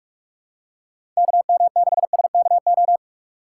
Morse code, heard as a keyed tone, about 46 wpm, 700 Hz. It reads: KM6SKO
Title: USA Call Signs - 8x Character Spacing - 45wpm